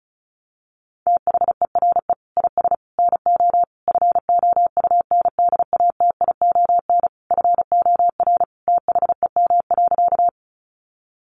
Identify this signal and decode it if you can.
Morse code, heard as a keyed tone, about 34 wpm, 700 Hz. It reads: THERE IS NO FOUNDATION FOR THEM.
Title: Sentences from Top 1200 Words 35wpm